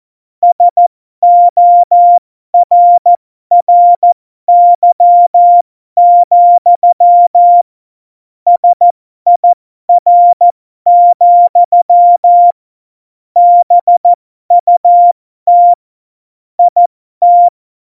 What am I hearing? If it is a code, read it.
Morse code, heard as a keyed tone, about 14 wpm, spaced as for 12 wpm, 700 Hz. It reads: SORRY, SIR, BUT IT
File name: Morsekurs / Witze